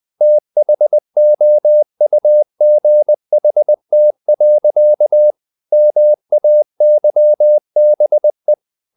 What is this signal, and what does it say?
Morse code, heard as a keyed tone, about 20 wpm, 600 Hz. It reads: THOUGHT. MAYBE
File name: the old man and the sea / the old man and the sea part 1 20wpm